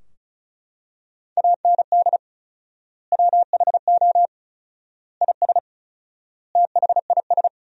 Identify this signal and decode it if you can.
Morse code, heard as a keyed tone, about 36 wpm, 700 Hz. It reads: AND WHO IS THIS
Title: 4x Word-Spacing - Sentences from Top 300 Words - 35wpm